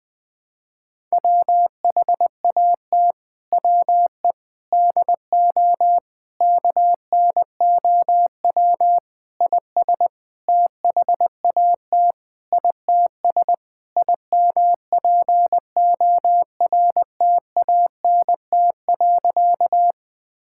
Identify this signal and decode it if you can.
Morse code, heard as a keyed tone, about 21 wpm, 700 Hz. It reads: WHAT WE DO KNOW IS THAT ITS IMPORTANT.
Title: Sentences with 6-Word Phrases - 20wpm